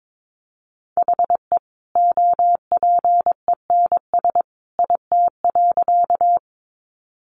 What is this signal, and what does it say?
Morse code, heard as a keyed tone, about 22 wpm, 700 Hz. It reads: HE OPENS IT.
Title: Additional Easy Phrases 22wpm